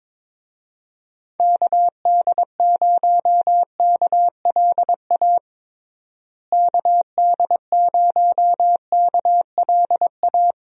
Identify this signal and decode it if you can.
Morse code, heard as a keyed tone, about 22 wpm, 700 Hz. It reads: KD0KLA KD0KLA
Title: USA Call Signs - 6 Characters - Repeated 3x - 22wpm